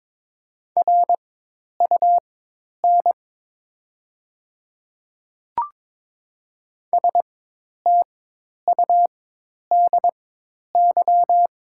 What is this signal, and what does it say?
Morse code, heard as a keyed tone, about 22 wpm, spaced as for 10 wpm, 700 Hz. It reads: RUN STUDY
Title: 4x Character Spacing - Top 300 Words - 22wpm